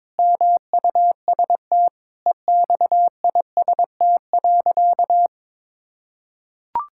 Morse code, heard as a keyed tone, about 21 wpm, 700 Hz. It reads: MUST EXIST.
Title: Sentences from Top 1000 Words - Review Part 5 - 22wpm